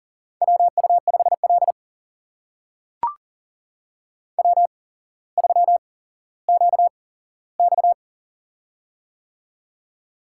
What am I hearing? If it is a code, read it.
Morse code, heard as a keyed tone, about 42 wpm, spaced as for 17 wpm, 700 Hz. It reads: WU5L W3QX
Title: USA Call Signs - 4 Characters - 8x Character Spacing - 40wpm